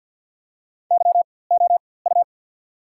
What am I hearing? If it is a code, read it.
Morse code, heard as a keyed tone, about 54 wpm, spaced as for 24 wpm, 690 Hz. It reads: YOU
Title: 4x Character Spacing - Top 300 Words - 50wpm